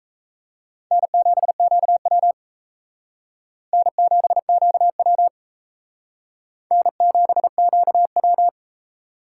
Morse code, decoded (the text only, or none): N7QW N7QW N7QW